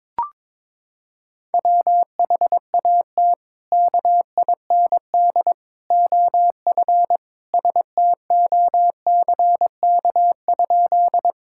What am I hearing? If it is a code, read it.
Morse code, heard as a keyed tone, about 21 wpm, 700 Hz. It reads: WHAT KIND OF STOCK?
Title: Sentences from Top 1000 Words - Review Part 5 - 22wpm